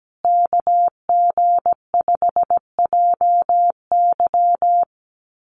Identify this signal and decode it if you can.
Morse code, heard as a keyed tone, about 17 wpm, 700 Hz. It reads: KG5JY
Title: USA Call Signs Repeated 3x 17wpm